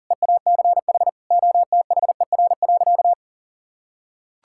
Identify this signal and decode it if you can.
Morse code, heard as a keyed tone, about 40 wpm, 700 Hz. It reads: EACH OTHER.
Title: Sentences from Top 1000 Words - Review Part 6 - 40wpm